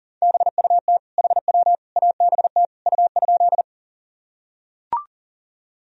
Morse code, heard as a keyed tone, about 40 wpm, 700 Hz. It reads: BUT HW ABT U?
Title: Sentences from Top 100 Words and Abbreviations 40wpm